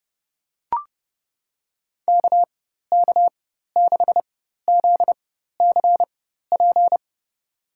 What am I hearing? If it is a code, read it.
Morse code, heard as a keyed tone, about 30 wpm, spaced as for 14 wpm, 700 Hz. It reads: KK6ZCP
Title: USA Call Signs - 6 Characters - 4x Character Spacing - 30wpm